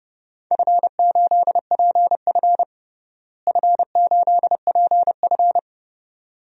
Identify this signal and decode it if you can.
Morse code, heard as a keyed tone, about 31 wpm, 700 Hz. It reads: F8PF F8PF
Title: DXCC-100 Call Signs - 4 Characters - Repeated 2x - 30wpm